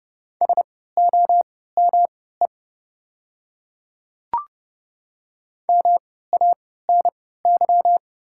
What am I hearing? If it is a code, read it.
Morse code, heard as a keyed tone, about 30 wpm, spaced as for 17 wpm, 700 Hz. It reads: SOME MANY
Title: 3x Character Spacing - Top 200 Words - 30wpm